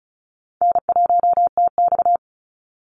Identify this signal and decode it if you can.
Morse code, heard as a keyed tone, about 34 wpm, 700 Hz. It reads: N1TX
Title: Call Signs Repeated 3x - 35wpm